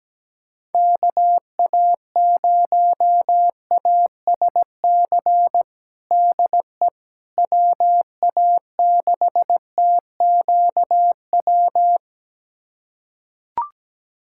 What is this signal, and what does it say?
Morse code, heard as a keyed tone, about 17 wpm, 700 Hz. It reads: KA0ASC DE WA6TQW
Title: USA Callsign DE Callsign - 6 Characters - 17wpm